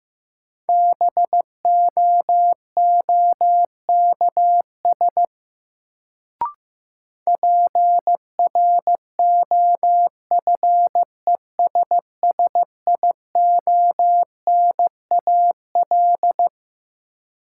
Morse code, heard as a keyed tone, about 15 wpm, 700 Hz. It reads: BOOKS PROFESSIONAL
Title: Sentences from Top 1000 Words - Review Part 4 - New Word Drill - 15wpm